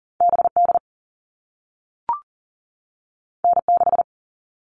Morse code, heard as a keyed tone, about 40 wpm, 700 Hz. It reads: BD N6
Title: All Cut Numbers - 2-Digit 40wpm